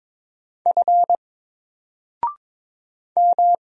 Morse code, heard as a keyed tone, about 22 wpm, 700 Hz. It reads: F M
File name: Single-Letter-Number-22wpm